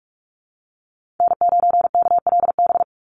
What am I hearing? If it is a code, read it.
Morse code, heard as a keyed tone, about 44 wpm, 700 Hz. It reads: N9KLB
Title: USA Call Signs 45wpm